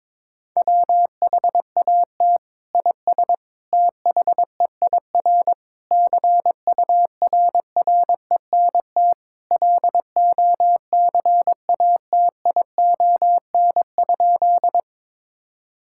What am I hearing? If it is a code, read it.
Morse code, heard as a keyed tone, about 22 wpm, 700 Hz. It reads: WHAT IS THEIR CURRENT LOCATION?
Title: Sentences from Top 1000 Words - Review Part 6 - 22wpm